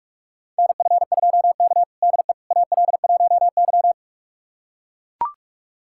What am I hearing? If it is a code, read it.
Morse code, heard as a keyed tone, about 42 wpm, 700 Hz. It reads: NF2K DE AL1Y
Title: USA Callsign DE Callsign - 4 Characters - 45wpm